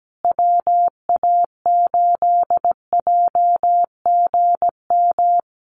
Morse code, heard as a keyed tone, about 17 wpm, 700 Hz. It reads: WA8JGM
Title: USA Call Signs Repeated 2x 17wpm